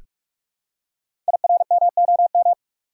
Morse code, heard as a keyed tone, about 45 wpm, 700 Hz. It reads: IRMOM
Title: Code Groups - 5 Letters - 45wpm